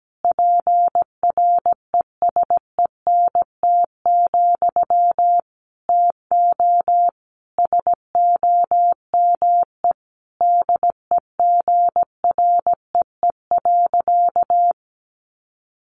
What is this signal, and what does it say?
Morse code, heard as a keyed tone, about 17 wpm, 700 Hz. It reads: PRESENT, TO SOME DEGREE.
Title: Sentences from Top 1500 Words 17wpm